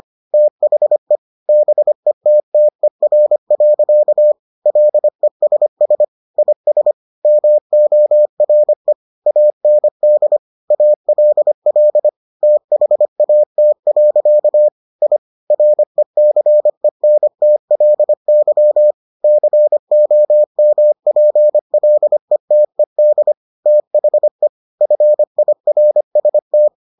Morse code, heard as a keyed tone, about 25 wpm, 600 Hz. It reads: THE BETTER. LESS IS MORE AND ALL THAT. I RECENTLY COMPLETED THE FIRST